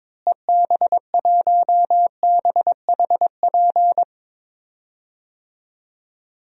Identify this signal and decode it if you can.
Morse code, heard as a keyed tone, about 22 wpm, 700 Hz. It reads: EB1BHP
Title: DXCC-100 Call Signs - 6 Characters - Repeated 2x - 22wpm